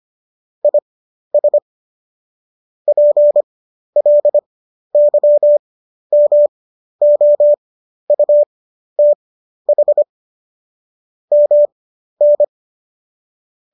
Morse code, heard as a keyed tone, about 25 wpm, spaced as for 12 wpm, 590 Hz. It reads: IS PLYMOUTH MN
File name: qso008 12